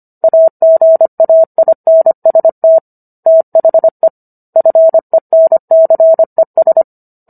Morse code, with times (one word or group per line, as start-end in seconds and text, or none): 0.2-2.8 AGAINST
3.3-4.1 THE
4.6-6.8 FENCES